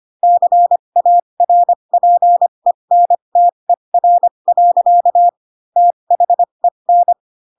Morse code, heard as a keyed tone, about 25 wpm, spaced as for 22 wpm, 700 Hz. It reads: CARPENTER. THEN
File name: Pinocchio / Pinocchio ch2 18 25 ik1hge